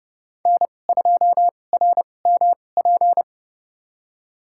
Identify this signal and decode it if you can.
Morse code, heard as a keyed tone, about 30 wpm, spaced as for 22 wpm, 700 Hz. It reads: N2RMP
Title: USA Call Signs - 5 Characters - 2x Character Spacing - 30wpm